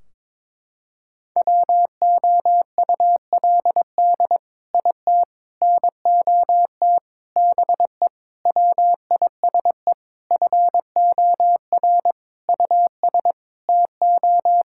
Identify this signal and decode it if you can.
Morse code, heard as a keyed tone, about 22 wpm, 700 Hz. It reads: WOULD IT NOT BE WISE FOR US TO
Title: Sentences from Top 1600 Words 22wpm